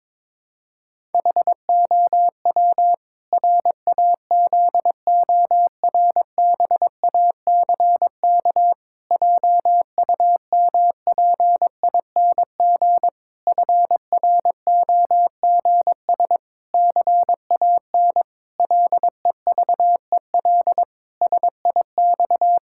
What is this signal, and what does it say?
Morse code, heard as a keyed tone, about 22 wpm, 700 Hz. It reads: HOW RAZORBACK JUMPING FROGS CAN LEVEL SIX
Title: Pangram Challenge v2 - 22wpm